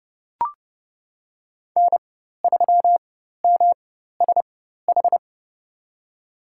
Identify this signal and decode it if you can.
Morse code, heard as a keyed tone, about 29 wpm, spaced as for 14 wpm, 700 Hz. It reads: N3MSH
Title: USA Call Signs - 5 Characters - 4x Character Spacing - 30wpm